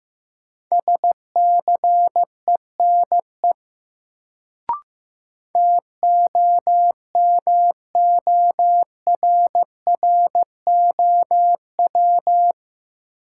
Morse code, heard as a keyed tone, about 15 wpm, 700 Hz. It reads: SCENE TOMORROW
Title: Sentences from Top 1100 Words - New Word Drill - 15wpm